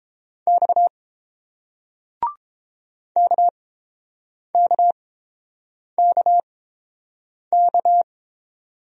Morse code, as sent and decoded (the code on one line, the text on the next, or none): -..- / -.- -.- -.- -.-
X KKKK